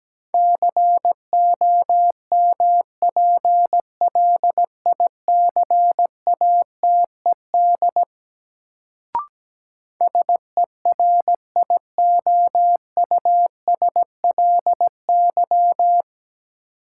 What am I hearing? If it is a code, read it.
Morse code, heard as a keyed tone, about 17 wpm, 700 Hz. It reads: COMPLICATED SERIOUSLY
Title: Sentences from Top 1200 Words - New Word Drill - 17wpm